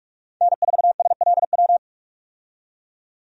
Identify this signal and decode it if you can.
Morse code, heard as a keyed tone, about 44 wpm, 700 Hz. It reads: N4SLW
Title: USA Call Signs - 5 Characters - 45wpm